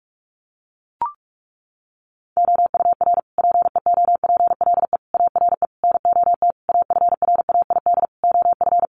Morse code, heard as a keyed tone, about 44 wpm, 700 Hz. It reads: OUR PEOPLE ARE NOT AFRAID OF
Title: Sentences from Top 1500 Words 45wpm